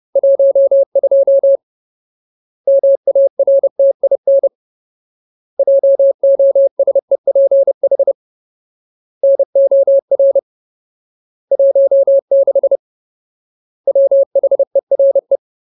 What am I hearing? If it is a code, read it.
Morse code, heard as a keyed tone, about 30 wpm, 550 Hz. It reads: 12 MARTIN JOSEPH NOR 16 WHERE
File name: Ultimate Word List / 30WPM Ultimate